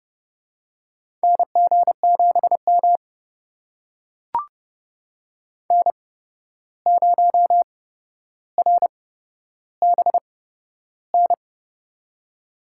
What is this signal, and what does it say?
Morse code, heard as a keyed tone, about 30 wpm, spaced as for 12 wpm, 700 Hz. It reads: NG7M N0RBN
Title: USA Call Signs - 8x Character Spacing - 30wpm